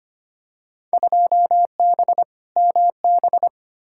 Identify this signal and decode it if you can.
Morse code, heard as a keyed tone, about 25 wpm, 700 Hz. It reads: 2B MB
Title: Field Day Exchanges Encore - 25wpm